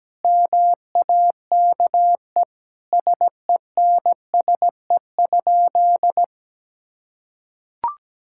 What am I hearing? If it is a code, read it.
Morse code, heard as a keyed tone, about 17 wpm, 700 Hz. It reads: MAKE SENSE?
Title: Sentences from Top 1000 Words - Review Part 5 - 17wpm